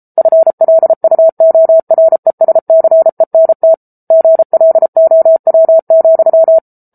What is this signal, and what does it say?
Morse code, heard as a keyed tone, about 33 wpm, 650 Hz. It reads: FLUORESCENT GLOW,
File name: Book Two / 07-Chapter 7 - 33 WPM